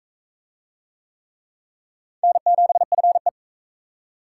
Morse code, heard as a keyed tone, about 40 wpm, 700 Hz. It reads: N7FE